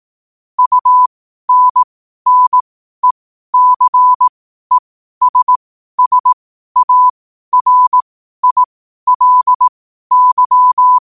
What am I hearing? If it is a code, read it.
Morse code, heard as a keyed tone, about 18 wpm, spaced as for 12 wpm, 1000 Hz. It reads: UNNECESSARILY
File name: qotdfiles / 12wpm